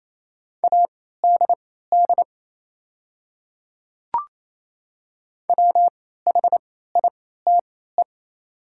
Morse code, heard as a keyed tone, about 28 wpm, spaced as for 16 wpm, 700 Hz. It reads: ADD WHITE
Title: 3x Character Spacing - Top 300 Words - 28wpm